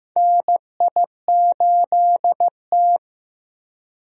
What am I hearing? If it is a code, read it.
Morse code, heard as a keyed tone, about 15 wpm, 700 Hz. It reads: NI8T
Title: USA Call Signs - 4 Characters - 15wpm